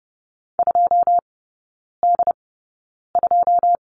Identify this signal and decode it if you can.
Morse code, heard as a keyed tone, about 30 wpm, spaced as for 9 wpm, 700 Hz. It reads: 2D2
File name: 081 Lesson 14e - Number 2 - Three Characters 30wpm